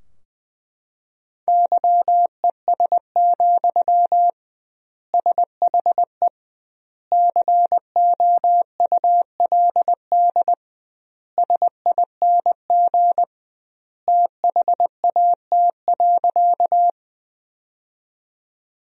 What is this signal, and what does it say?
Morse code, heard as a keyed tone, about 20 wpm, 700 Hz. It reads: YES, SHE COULD SING THAT.
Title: 2x Word-Spacing - Sentences from Top 500 Words - 20wpm